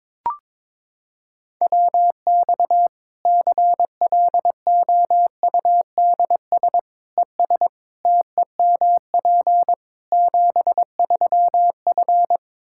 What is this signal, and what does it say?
Morse code, heard as a keyed tone, about 22 wpm, 700 Hz. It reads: WX CLOUDS ES TEMP 73F
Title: Practice QSO - Exchange 2 - Info - 22wpm